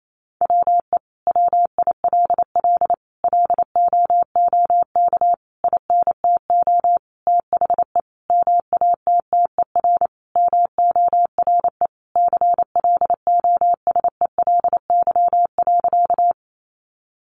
Morse code, heard as a keyed tone, about 28 wpm, 700 Hz. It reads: WE WILL LOOK INTO THE MATTER MORE CLOSELY.
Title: Sentences from Top 1400 Words 28wpm